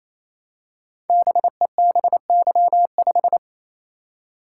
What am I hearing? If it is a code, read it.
Morse code, heard as a keyed tone, about 27 wpm, 700 Hz. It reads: BEBY5
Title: Code Groups - 5 Letter:Numbers - 28wpm